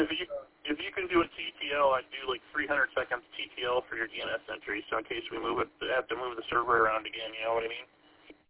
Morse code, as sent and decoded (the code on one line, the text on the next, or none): none